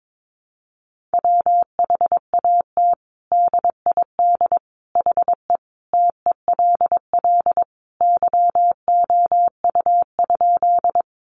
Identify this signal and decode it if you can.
Morse code, heard as a keyed tone, about 22 wpm, 700 Hz. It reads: WHAT DID HE TELL YOU?
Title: Sentences from Top 200 Words 22wpm